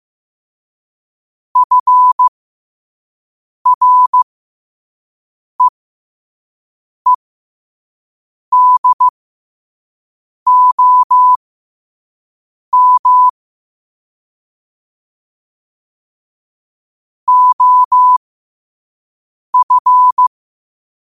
Morse code, heard as a keyed tone, about 15 wpm, spaced as for 5 wpm, 1000 Hz. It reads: FREEDOM OF